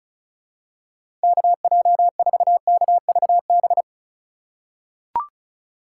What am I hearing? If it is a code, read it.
Morse code, heard as a keyed tone, about 36 wpm, 700 Hz. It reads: KJ4KVB
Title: USA Call Signs - 6 Characters - 3x Character Spacing - 35wpm